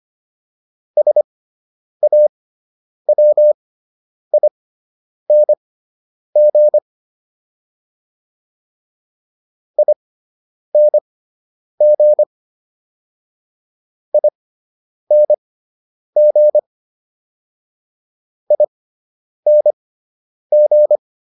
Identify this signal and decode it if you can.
Morse code, heard as a keyed tone, about 25 wpm, spaced as for 9 wpm, 600 Hz. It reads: SAWING ING ING ING
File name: ING6 09